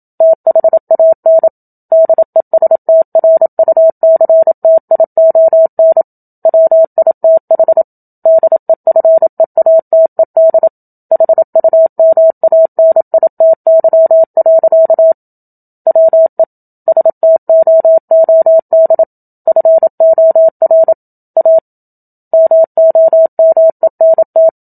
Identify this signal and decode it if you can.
Morse code, heard as a keyed tone, about 27 wpm, 650 Hz. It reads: THAN DESTRUCTION WITH DEFEATED HUMANITY. WE STOOD FOR A MOMENT
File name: Book Two / 01-Chapter 1 - 27 WPM